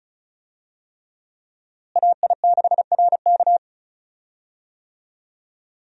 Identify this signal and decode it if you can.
Morse code, heard as a keyed tone, about 36 wpm, 700 Hz. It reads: AI6RK
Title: USA Call Signs - 5 Characters - Repeated 3x - 35wpm